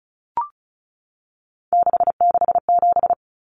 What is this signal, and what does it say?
Morse code, heard as a keyed tone, about 34 wpm, 700 Hz. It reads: 667
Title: Common Cut Numbers - 3-Digit 35wpm